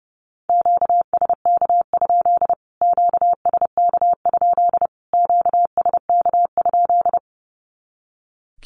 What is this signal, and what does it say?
Morse code, heard as a keyed tone, about 30 wpm, 700 Hz. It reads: QSK? QSK? QSK?
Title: Q-Codes Repeated 3x 30wpm